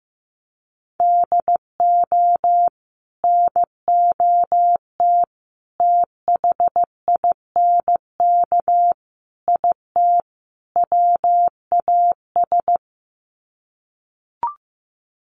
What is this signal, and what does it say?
Morse code, heard as a keyed tone, about 15 wpm, 700 Hz. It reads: DO NOT THINK IT WAS
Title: Sets of 5 Words Encore 15wpm